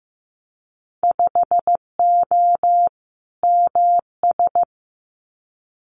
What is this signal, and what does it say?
Morse code, heard as a keyed tone, about 15 wpm, 700 Hz. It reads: 5O MS
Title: Winter Field Day Exchanges Encore 15wpm